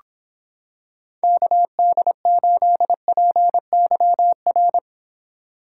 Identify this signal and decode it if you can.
Morse code, heard as a keyed tone, about 26 wpm, 700 Hz. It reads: KD8PYR